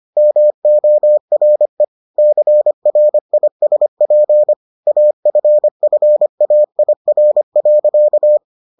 Morse code, heard as a keyed tone, about 25 wpm, 600 Hz. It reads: MORE CRISP AFFAIR.